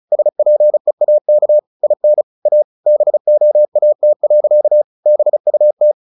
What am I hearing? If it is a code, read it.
Morse code, heard as a keyed tone, about 34 wpm, 600 Hz. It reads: SPEAK IN A BOAT. BUT
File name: the old man and the sea / the old man and the sea 1 35wpm